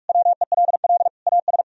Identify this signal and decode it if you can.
Morse code, heard as a keyed tone, about 58 wpm, 700 Hz. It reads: WELL AS